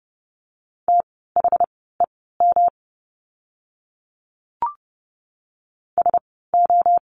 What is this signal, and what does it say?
Morse code, heard as a keyed tone, about 30 wpm, spaced as for 17 wpm, 700 Hz. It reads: THEM SO
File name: course / 30:10wpm / 108 Lesson 18f - Letter m - Words 30wpm